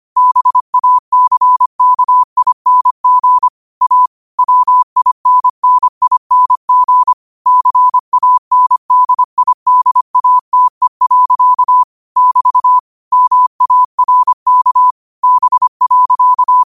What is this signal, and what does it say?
Morse code, heard as a keyed tone, about 25 wpm, 1000 Hz. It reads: BACKING A WINNING CANDIDATE. <BT> MARK B.